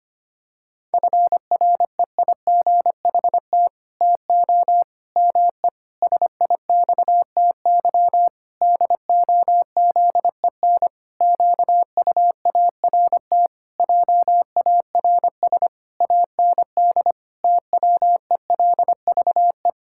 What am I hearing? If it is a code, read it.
Morse code, heard as a keyed tone, about 25 wpm, 700 Hz. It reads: FREIGHT TO ME SIXTY DOZEN QUART JARS AND TWELVE
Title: Pangram Challenge v2 - 25wpm